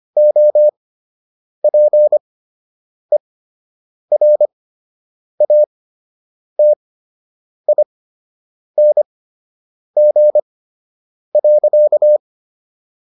Morse code, heard as a keyed tone, about 25 wpm, spaced as for 8 wpm, 600 Hz. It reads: OPERATING.